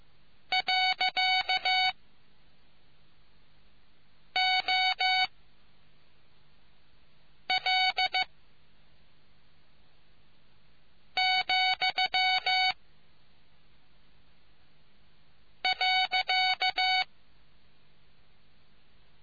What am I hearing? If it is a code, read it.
Morse code, heard as a keyed tone, about 15 wpm, spaced as for 3 wpm, 750 Hz. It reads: .OL,.